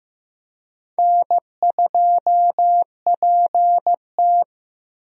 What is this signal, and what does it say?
Morse code, heard as a keyed tone, about 15 wpm, 700 Hz. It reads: N2PT
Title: USA Call Signs - 4 Characters - Repeated 2x - 15wpm